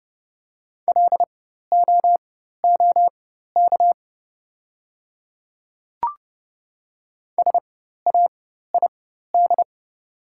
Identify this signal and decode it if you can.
Morse code, heard as a keyed tone, about 29 wpm, spaced as for 14 wpm, 700 Hz. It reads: LOOK SAID
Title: 4x Character Spacing - Top 300 Words - 30wpm